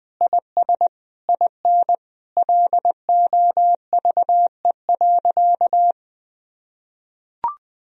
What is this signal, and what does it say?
Morse code, heard as a keyed tone, about 21 wpm, 700 Hz. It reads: IS IN LOVE.
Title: Sentences from Top 1000 Words - Review Part 3 - 20wpm